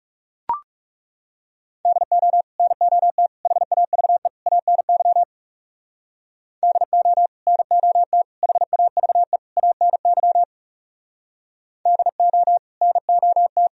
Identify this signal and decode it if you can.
Morse code, heard as a keyed tone, about 39 wpm, 700 Hz. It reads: DO NOT HAVE ANY DO NOT HAVE ANY DO NOT